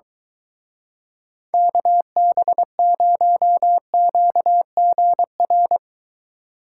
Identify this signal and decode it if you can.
Morse code, heard as a keyed tone, about 22 wpm, 700 Hz. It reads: KB0QGR